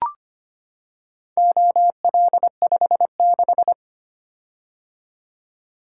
Morse code, heard as a keyed tone, about 25 wpm, 700 Hz. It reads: OL56
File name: practice / 25wpm / Four Letter-Number Combinations - 25wpm